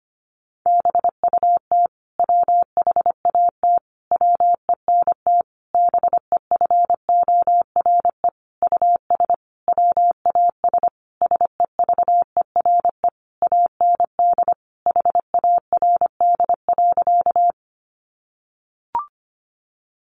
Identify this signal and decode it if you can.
Morse code, heard as a keyed tone, about 25 wpm, 700 Hz. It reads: BUT WHAT WENT BEFORE US WAS SEVERE AND HARD.
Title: Sentences from Top 1400 Words 25wpm